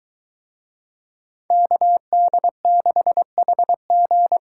Morse code, heard as a keyed tone, about 23 wpm, 700 Hz. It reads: KD6HG